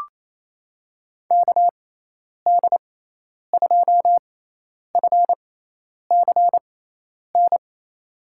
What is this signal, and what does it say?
Morse code, heard as a keyed tone, about 28 wpm, spaced as for 10 wpm, 700 Hz. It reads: KD2FCN
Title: USA Call Signs - 6x Character Spacing - 28wpm